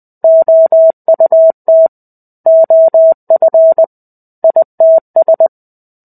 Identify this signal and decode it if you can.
Morse code, heard as a keyed tone, about 20 wpm, 650 Hz. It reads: OUT OF ITS
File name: 05-Chapter-05 20 WPM